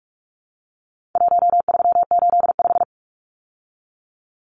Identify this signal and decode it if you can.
Morse code, heard as a keyed tone, about 44 wpm, 700 Hz. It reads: 1385
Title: Numbers - 4-Digit 45wpm